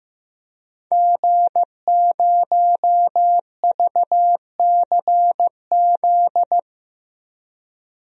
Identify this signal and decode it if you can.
Morse code, heard as a keyed tone, about 15 wpm, 700 Hz. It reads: G0VCZ
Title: DXCC-100 Call Signs - 5 Characters - 15wpm